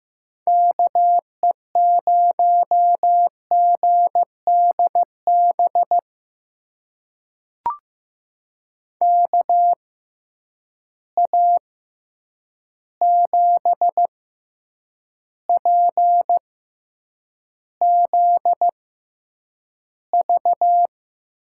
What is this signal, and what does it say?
Morse code, heard as a keyed tone, about 15 wpm, spaced as for 8 wpm, 700 Hz. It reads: KE0GDB KA7PZV
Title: USA Call Signs - 6 Characters - 6x Character Spacing - 15wpm